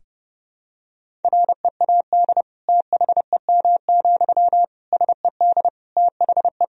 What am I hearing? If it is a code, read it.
Morse code, heard as a keyed tone, about 30 wpm, 700 Hz. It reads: READ THEM, SED THE
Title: Sentences from Top 400 Words and Abbreviations 30wpm